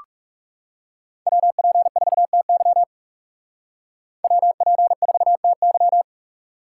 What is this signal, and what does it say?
Morse code, heard as a keyed tone, about 36 wpm, 700 Hz. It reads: WP4TY WP4TY
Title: Speed Racing - USA 2x2 Call Signs 30wpm